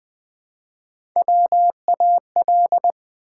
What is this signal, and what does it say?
Morse code, heard as a keyed tone, about 20 wpm, 700 Hz. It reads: WAL
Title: Build A Word - 1 Letter At A Time - 4-Letter Words - 20wpm